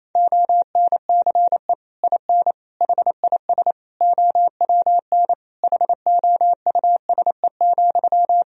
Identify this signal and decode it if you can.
Morse code, heard as a keyed tone, about 28 wpm, 700 Hz. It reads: ONCE IN HIS OWN HOUSE,
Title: Sentences from Top 1000 Words - Review Part 2 - 28wpm